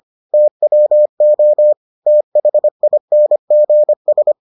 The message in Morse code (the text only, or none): TWO THINGS